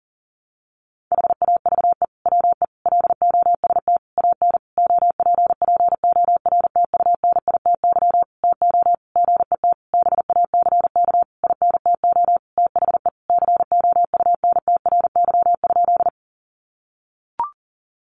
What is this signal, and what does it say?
Morse code, heard as a keyed tone, about 40 wpm, 700 Hz. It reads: HAVE WE LOST AN OPPORTUNITY TO GET BACK INTO THE COUNTRY?
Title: Sentences from Top 1400 Words 40wpm